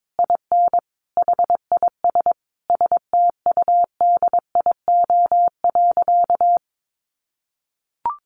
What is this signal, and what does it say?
Morse code, heard as a keyed tone, about 22 wpm, 700 Hz. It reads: IN HIS STUDIO.
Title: Sentences from Top 1200 Words 22wpm